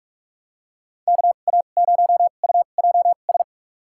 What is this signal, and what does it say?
Morse code, heard as a keyed tone, about 42 wpm, spaced as for 31 wpm, 700 Hz. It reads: KA0UJS
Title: USA Call Signs - 2x Character Spacing - 45wpm